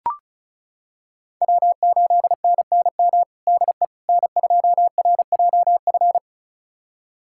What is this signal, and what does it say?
Morse code, heard as a keyed tone, about 34 wpm, 700 Hz. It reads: W8NNM DE N2RJF
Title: USA Callsign DE Callsign - 5 Characters - 35wpm